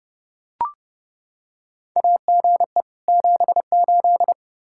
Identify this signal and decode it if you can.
Morse code, heard as a keyed tone, about 30 wpm, 700 Hz. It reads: AGE 78
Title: K1USN SST QSO Practice 30wpm